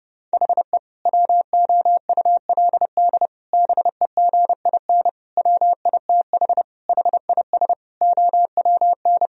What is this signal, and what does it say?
Morse code, heard as a keyed tone, about 30 wpm, 700 Hz. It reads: HE WOULD BEGIN WITH HIS OWN
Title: Sentences from Top 1000 Words - Review Part 1 - 30wpm